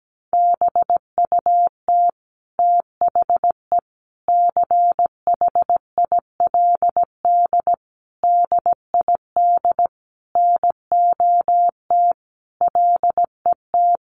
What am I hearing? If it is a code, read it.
Morse code, heard as a keyed tone, about 17 wpm, 700 Hz. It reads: BUT THE CHILD DID NOT LET
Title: Sentences from Top 1400 Words 17wpm